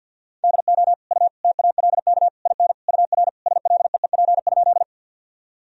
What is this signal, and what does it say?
Morse code, heard as a keyed tone, about 52 wpm, 700 Hz. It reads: DO U TALK IN UR SLEEP?
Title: Sentences from Top 500 Words and Abbreviations 50wpm